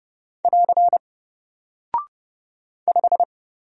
Morse code, heard as a keyed tone, about 30 wpm, 700 Hz. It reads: <AR> 5
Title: Single Character (Letter:Number:Punctuation:Prosign) Alt 30wpm